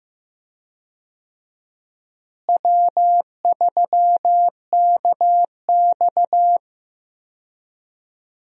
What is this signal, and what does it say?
Morse code, heard as a keyed tone, about 15 wpm, 700 Hz. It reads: W3KX
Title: USA Call Signs - 4 Characters - Repeated 2x - 15wpm